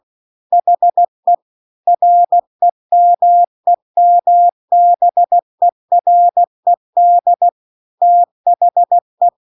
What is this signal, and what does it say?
Morse code, heard as a keyed tone, about 16 wpm, 700 Hz. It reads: HE REMEMBERED THE